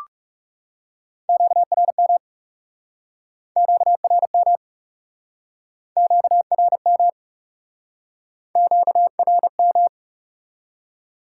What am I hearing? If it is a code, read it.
Morse code, heard as a keyed tone, about 41 wpm, 700 Hz. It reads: QRM QRM QRM QRM